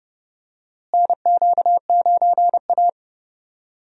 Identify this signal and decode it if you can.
Morse code, heard as a keyed tone, about 30 wpm, 700 Hz. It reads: NQ9A
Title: USA Call Signs - 4 Characters - Repeated 2x - 30wpm